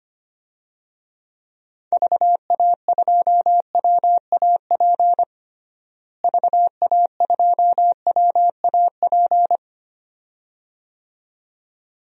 Morse code, heard as a keyed tone, about 25 wpm, 700 Hz. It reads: VA2WAP VA2WAP
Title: DXCC-100 Call Signs - 6 Characters - Repeated 2x - 25wpm